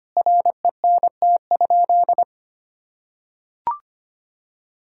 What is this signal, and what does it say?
Morse code, heard as a keyed tone, about 25 wpm, 700 Hz. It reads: RENT?
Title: Sentences from Top 1000 Words - Review Part 6 - 25wpm